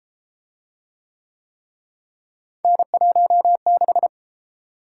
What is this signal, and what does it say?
Morse code, heard as a keyed tone, about 32 wpm, 700 Hz. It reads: N16